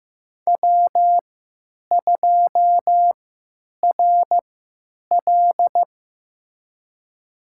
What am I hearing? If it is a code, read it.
Morse code, heard as a keyed tone, about 15 wpm, spaced as for 8 wpm, 700 Hz. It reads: W2RL
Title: USA Call Signs - 3x Character Spacing - 15wpm